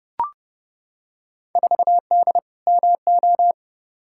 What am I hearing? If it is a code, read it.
Morse code, heard as a keyed tone, about 29 wpm, 700 Hz. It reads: 4D MO
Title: Field Day Exchanges Encore - 30wpm